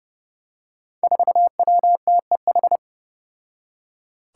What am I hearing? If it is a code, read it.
Morse code, heard as a keyed tone, about 30 wpm, 700 Hz. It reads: 4WTEH
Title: Five Challenging Letter-Number Combinations 30wpm